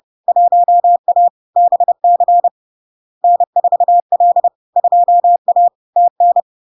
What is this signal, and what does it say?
Morse code, heard as a keyed tone, about 30 wpm, 700 Hz. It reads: 1A BC N4L 2A TN
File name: LARK-FD-2022-Log-in-CW-30wpm-700-Hz